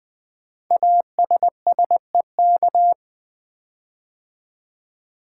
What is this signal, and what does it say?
Morse code, heard as a keyed tone, about 20 wpm, 700 Hz. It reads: ASSEK
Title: Code Groups - 5 Letters - 20wpm